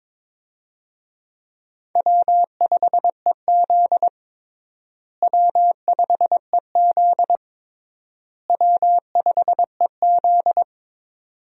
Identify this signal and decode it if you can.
Morse code, heard as a keyed tone, about 22 wpm, 700 Hz. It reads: W5EZ W5EZ W5EZ
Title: USA Call Signs - 4 Characters - Repeated 3x - 22wpm